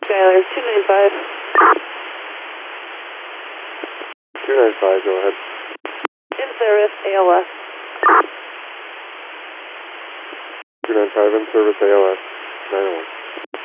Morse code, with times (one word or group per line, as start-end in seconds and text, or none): none